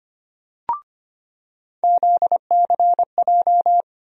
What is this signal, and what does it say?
Morse code, heard as a keyed tone, about 25 wpm, 700 Hz. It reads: ZCJ
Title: USA Call Sign Postfixes - 3 Letters 25wpm